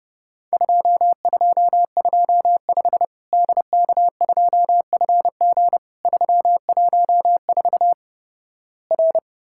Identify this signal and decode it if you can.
Morse code, heard as a keyed tone, about 29 wpm, 700 Hz. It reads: 2225 DK2FG 314 R
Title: Worked All Europe DX Contest - QTC Exchange - 30wpm